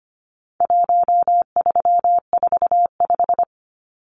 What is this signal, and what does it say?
Morse code, heard as a keyed tone, about 25 wpm, 700 Hz. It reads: 1345
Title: Numbers - 4-Digit 25wpm